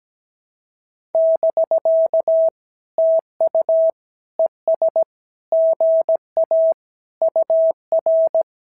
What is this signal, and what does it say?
Morse code, heard as a keyed tone, about 17 wpm, 650 Hz. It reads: <BK> TU ES GA UR
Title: SOTA QSO Practice 17wpm